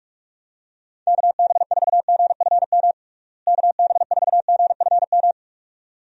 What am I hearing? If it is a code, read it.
Morse code, heard as a keyed tone, about 46 wpm, 700 Hz. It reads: KB4GFM KB4GFM
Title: USA Call Signs - 6 Characters - Repeated 2x - 45wpm